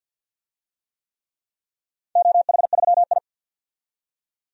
Morse code, heard as a keyed tone, about 51 wpm, 700 Hz. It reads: OH3I
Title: DXCC-100 Call Signs - 4 Characters - 50wpm